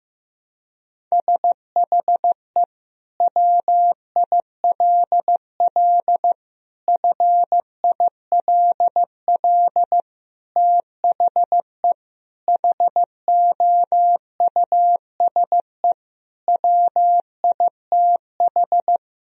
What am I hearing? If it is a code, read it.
Morse code, heard as a keyed tone, about 15 wpm, 700 Hz. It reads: SHE WILL FILL THE HOUSE WITH
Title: Sentences from Top 1000 Words - Review Part 3 - 15wpm